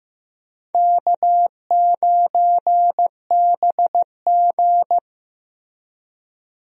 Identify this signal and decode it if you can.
Morse code, heard as a keyed tone, about 15 wpm, 700 Hz. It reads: K9BG
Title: USA Call Signs - 4 Characters - Repeated 2x - 15wpm